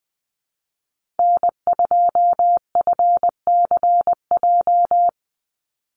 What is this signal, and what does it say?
Morse code, heard as a keyed tone, about 19 wpm, 700 Hz. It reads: N2FCJ
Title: USA Call Signs 20wpm